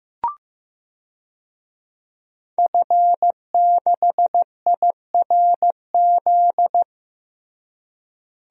Morse code, heard as a keyed tone, about 15 wpm, 700 Hz. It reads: F6IRZ
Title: DXCC-100 Call Signs - 5 Characters - Repeated 2x - 15wpm